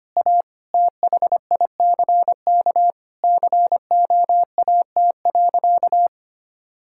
Morse code, heard as a keyed tone, about 25 wpm, 700 Hz. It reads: A THICK COAT.
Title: Sentences from Top 1600 Words 25wpm